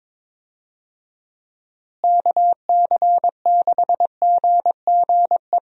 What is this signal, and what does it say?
Morse code, heard as a keyed tone, about 21 wpm, 700 Hz. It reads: KC6GGE
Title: USA Call Signs - 6 Characters - Repeated 3x - 22wpm